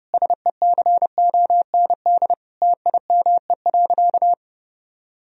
Morse code, heard as a keyed tone, about 30 wpm, 700 Hz. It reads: SECOND TIME.
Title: Sentences from Top 1000 Words - Review Part 1 - 30wpm